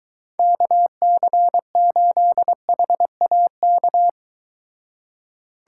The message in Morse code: KC8HAK

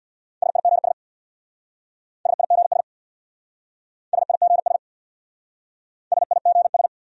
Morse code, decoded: SIGS SIGS SIGS SIGS

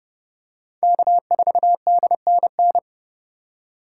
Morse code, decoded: K4DNN

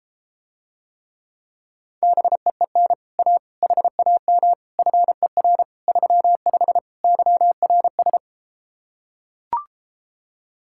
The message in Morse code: BEEN A HAM FER 35 YRS

